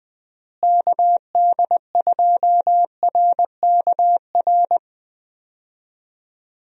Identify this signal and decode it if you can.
Morse code, heard as a keyed tone, about 20 wpm, 700 Hz. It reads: KD2RKR
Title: USA Call Signs - 6 Characters - Repeated 2x - 20wpm